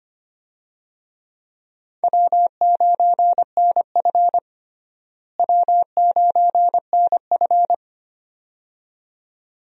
Morse code, heard as a keyed tone, about 25 wpm, 700 Hz. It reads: W9NF W9NF
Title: USA Call Signs - 4 Characters - Repeated 2x - 25wpm